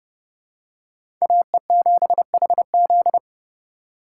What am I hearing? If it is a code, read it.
Morse code, heard as a keyed tone, about 30 wpm, 700 Hz. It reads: AE7HZ
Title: USA Call Signs - 5 Characters - 3x Character Spacing - 30wpm